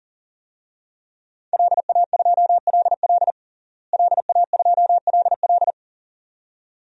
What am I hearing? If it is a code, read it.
Morse code, heard as a keyed tone, about 40 wpm, 700 Hz. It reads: LA2LL LA2LL
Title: DXCC-100 Call Signs - 5 Characters - Repeated 2x - 40wpm